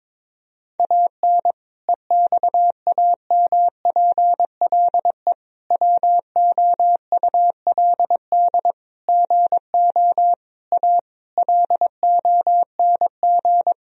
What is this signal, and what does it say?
Morse code, heard as a keyed tone, about 22 wpm, 700 Hz. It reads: AN EXAMPLE WOULD GO A LONG
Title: Sentences from Top 1000 Words - Review Part 1 - 22wpm